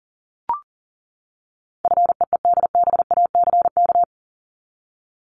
Morse code, heard as a keyed tone, about 38 wpm, 700 Hz. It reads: FEEDBACK
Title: Top 1501-1750 Words 40wpm